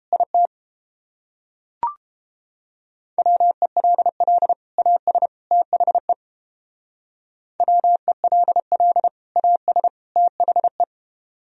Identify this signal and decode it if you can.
Morse code, heard as a keyed tone, about 29 wpm, 700 Hz. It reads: IT WELL AS THE WELL AS THE